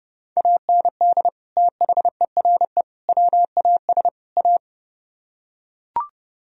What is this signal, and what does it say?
Morse code, heard as a keyed tone, about 32 wpm, 700 Hz. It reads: AND THERE WAS A